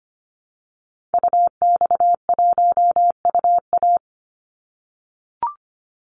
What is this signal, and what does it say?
Morse code, heard as a keyed tone, about 25 wpm, 700 Hz. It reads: UX1UA
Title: Call Signs - 25wpm